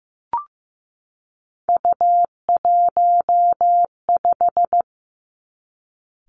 Morse code, heard as a keyed tone, about 15 wpm, 700 Hz. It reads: U15
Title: Three Challenging Letter-Number Combinations 15wpm